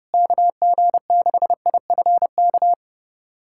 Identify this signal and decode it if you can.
Morse code, heard as a keyed tone, about 32 wpm, 700 Hz. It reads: KG6IFK
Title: USA Callsign DE Callsign 30wpm